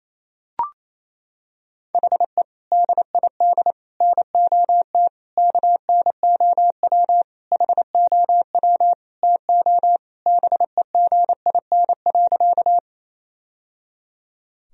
Morse code, heard as a keyed tone, about 28 wpm, 700 Hz. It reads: HE DID NOT KNOW HOW TO BEGIN.
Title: Sentences from Top 1000 Words - Review Part 1 - 28wpm